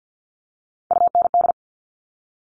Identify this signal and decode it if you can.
Morse code, heard as a keyed tone, about 50 wpm, 700 Hz. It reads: UND
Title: All Cut Numbers - 3-Digit 50wpm